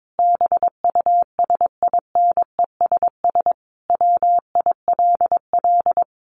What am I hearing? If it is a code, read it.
Morse code, heard as a keyed tone, about 22 wpm, 700 Hz. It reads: BUSINESS WILL
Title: Sentences from Top 1400 Words 22wpm